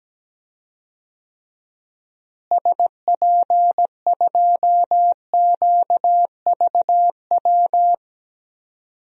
Morse code, heard as a keyed tone, about 17 wpm, 700 Hz. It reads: SP2QVW
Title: DXCC-100 Call Signs - 6 Characters - Repeated 3x - 17wpm